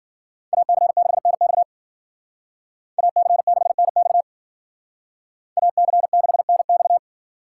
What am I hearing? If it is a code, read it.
Morse code, heard as a keyed tone, about 49 wpm, 700 Hz. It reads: AC6NX AC6NX AC6NX
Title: Speed Racing - USA 2x2 Call Signs 40wpm